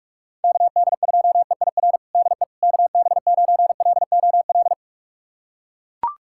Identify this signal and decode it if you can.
Morse code, heard as a keyed tone, about 41 wpm, 700 Hz. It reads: KD2EIR DE KB9LOL
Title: USA Callsign DE Callsign - 6 Characters - 45wpm